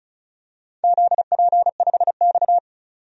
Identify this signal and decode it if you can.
Morse code, heard as a keyed tone, about 36 wpm, 700 Hz. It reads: ZP5X
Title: DXCC-100 Call Signs - 4 Characters - 35wpm